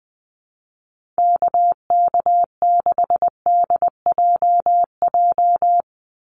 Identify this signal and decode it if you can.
Morse code, heard as a keyed tone, about 20 wpm, 700 Hz. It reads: KK6DJJ
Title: USA Call Signs 20wpm